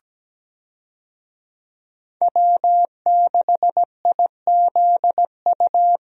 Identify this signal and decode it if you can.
Morse code, heard as a keyed tone, about 17 wpm, 700 Hz. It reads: W6IZU
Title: USA Call Signs - 5 Characters - 17wpm